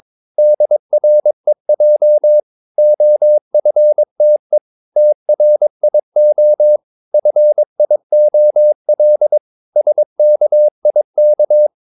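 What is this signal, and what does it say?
Morse code, heard as a keyed tone, about 22 wpm, 600 Hz. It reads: DREJ OFTE TRIO FIOL SKIK
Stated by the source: Morse practice of short Danish words at 22 wpm, from CW med Gnister.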